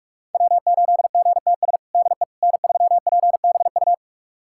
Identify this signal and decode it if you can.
Morse code, heard as a keyed tone, about 43 wpm, 700 Hz. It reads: W8GTS DE N3PBU
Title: USA Callsign DE Callsign - 5 Characters - 45wpm